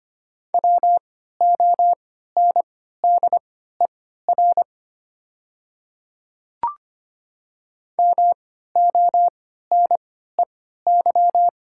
Morse code, heard as a keyed tone, about 25 wpm, spaced as for 14 wpm, 690 Hz. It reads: WONDER MONEY
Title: 3x Character Spacing - Top 500 Words - 25wpm